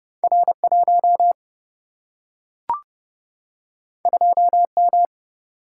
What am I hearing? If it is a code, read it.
Morse code, heard as a keyed tone, about 29 wpm, 700 Hz. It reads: R1 2M
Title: Single-Letter Single-Number 30wpm